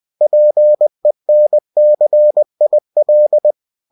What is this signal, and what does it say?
Morse code, heard as a keyed tone, about 20 wpm, 600 Hz. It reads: PENCIL